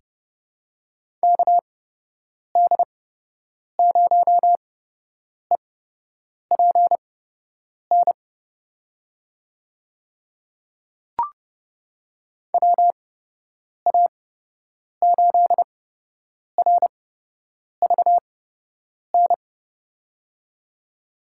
Morse code, heard as a keyed tone, about 31 wpm, spaced as for 8 wpm, 700 Hz. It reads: KD0EPN WA8RVN
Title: USA Call Signs - 6 Characters - 8x Character Spacing - 30wpm